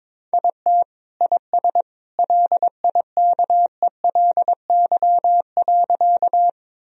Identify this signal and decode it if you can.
Morse code, heard as a keyed tone, about 22 wpm, 700 Hz. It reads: IT IS LIKELY.
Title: Sentences from Top 1000 Words - Review Part 4 - 22wpm